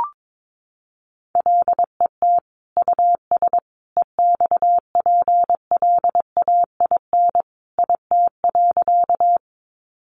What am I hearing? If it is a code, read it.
Morse code, heard as a keyed tone, about 22 wpm, 700 Hz. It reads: LET US EXPLAIN IT.
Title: Sentences from Top 1100 Words 22wpm